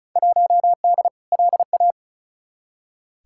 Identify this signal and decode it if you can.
Morse code, heard as a keyed tone, about 36 wpm, 700 Hz. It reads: 1D LA
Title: Field Day Exchanges - 35wpm